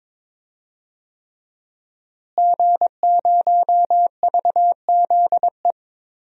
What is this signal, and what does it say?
Morse code, heard as a keyed tone, about 22 wpm, 700 Hz. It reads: G0VZE
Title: DXCC-100 Call Signs - 5 Characters - Repeated 3x - 22wpm